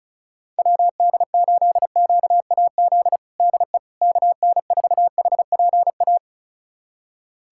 Morse code, heard as a keyed tone, about 34 wpm, 700 Hz. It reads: WD8QAZ DE KN4HPA
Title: USA Callsign DE Callsign - 6 Characters - 35wpm